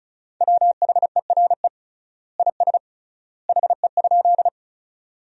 Morse code, heard as a keyed tone, about 36 wpm, 700 Hz. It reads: WHERE IS HE?
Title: 3x Word-Spacing - Sentences from Top 200 Words - 35wpm